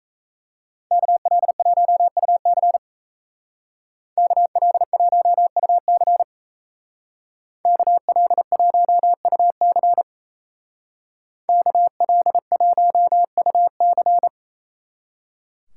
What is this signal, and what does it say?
Morse code, heard as a keyed tone, about 36 wpm, 700 Hz. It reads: KL1UC KL1UC KL1UC KL1UC